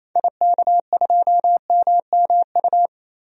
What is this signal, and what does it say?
Morse code, heard as a keyed tone, about 28 wpm, 700 Hz. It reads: IK2MMU
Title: DXCC-100 Call Signs - 6 Characters - 28wpm